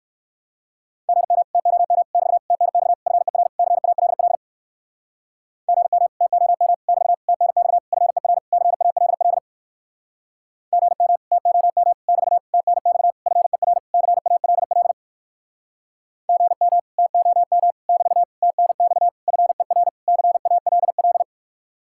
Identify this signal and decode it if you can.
Morse code, heard as a keyed tone, about 64 wpm, 700 Hz. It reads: GM TOM <BT> TNX FER CALL GM TOM <BT> TNX FER CALL GM TOM <BT> TNX FER CALL GM TOM <BT> TNX FER CALL